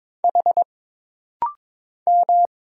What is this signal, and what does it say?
Morse code, heard as a keyed tone, about 22 wpm, 700 Hz. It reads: H M
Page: This rapid-fire timing is geared to push your ICR (Instant Character Recognition) skills.